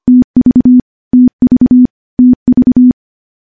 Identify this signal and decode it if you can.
Morse code, heard as a keyed tone, about 25 wpm, 250 Hz. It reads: TV TV TV
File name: morse-training-25WPM